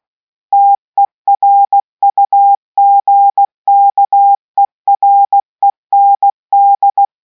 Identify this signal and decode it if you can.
Morse code, heard as a keyed tone, about 16 wpm, 800 Hz. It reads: TERUGKEREND